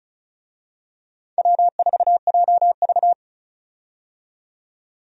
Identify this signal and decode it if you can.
Morse code, heard as a keyed tone, about 34 wpm, 700 Hz. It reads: W4JV
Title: USA Call Signs - 4 Characters - 35wpm